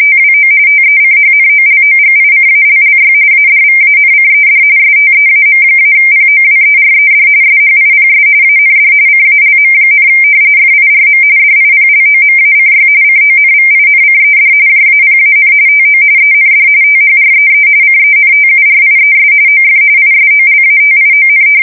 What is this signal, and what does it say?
RTTY, about 45.45 baud. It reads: TWO TECHNOLOGICALLY ADVANCED STEAM-POWERED SHIPS AND CALM, FIRM DIPLOMACY CONVINCED JAPAN TO END THREE CENTURIES OF ISOLATION
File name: fsk170 / usnavy2